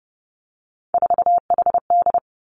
Morse code, heard as a keyed tone, about 30 wpm, 700 Hz. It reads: 4HD
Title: Three Challenging Letter-Number Combinations 30wpm